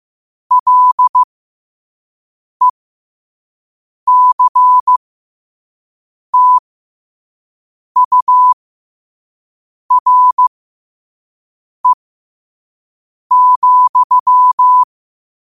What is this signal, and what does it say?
Morse code, heard as a keyed tone, about 15 wpm, spaced as for 5 wpm, 1000 Hz. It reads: LECTURE,
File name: qotdfiles / 5wpm